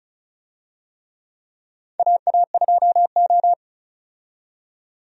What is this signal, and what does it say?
Morse code, heard as a keyed tone, about 34 wpm, 700 Hz. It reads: AA2O
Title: USA Call Signs - 4 Characters - 35wpm